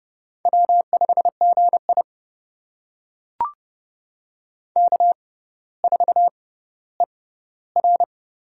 Morse code, heard as a keyed tone, about 30 wpm, spaced as for 15 wpm, 700 Hz. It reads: W5GI K4ER
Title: USA Call Signs - 4 Characters - 6x Character Spacing - 30wpm